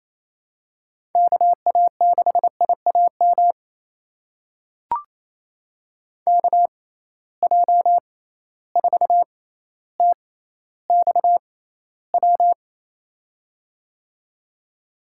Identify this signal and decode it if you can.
Morse code, heard as a keyed tone, about 27 wpm, spaced as for 14 wpm, 700 Hz. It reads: KA6IAM KJ4TXW
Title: USA Call Signs - 6 Characters - 6x Character Spacing - 28wpm